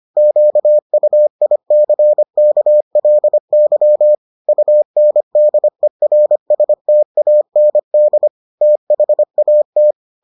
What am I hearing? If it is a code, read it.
Morse code, heard as a keyed tone, about 25 wpm, 600 Hz. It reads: QUICKLY UNDERSTAND THAT